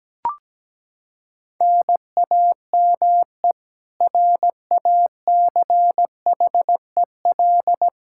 Morse code, heard as a keyed tone, about 17 wpm, 700 Hz. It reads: NAME RACHEL
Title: Practice QSO - Exchange 1 - Beginning - 17wpm